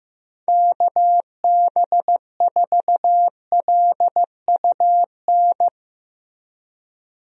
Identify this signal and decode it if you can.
Morse code, heard as a keyed tone, about 16 wpm, 700 Hz. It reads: KB4LUN
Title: USA Call Signs - 6 Characters - Repeated 2x - 15wpm